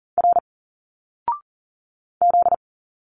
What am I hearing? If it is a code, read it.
Morse code, heard as a keyed tone, about 40 wpm, 700 Hz. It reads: R Z
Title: Single Letters 40wpm